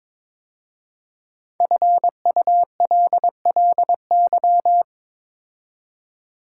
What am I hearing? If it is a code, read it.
Morse code, heard as a keyed tone, about 22 wpm, 700 Hz. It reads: FULLY
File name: 5 Letter Words - First 3 Letters Given - 22wpm